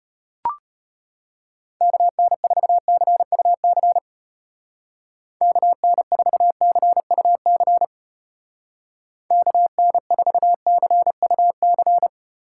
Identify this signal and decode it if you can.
Morse code, heard as a keyed tone, about 31 wpm, 700 Hz. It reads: KN4CUC KN4CUC KN4CUC
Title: Speed Racing - USA 2x3 Call Signs 25wpm